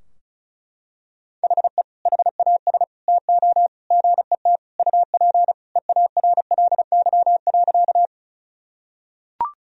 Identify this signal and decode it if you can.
Morse code, heard as a keyed tone, about 36 wpm, 700 Hz. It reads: HE HAS TO GET UP EARLY.
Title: Sentences from Top 1000 Words - Review Part 2 - 35wpm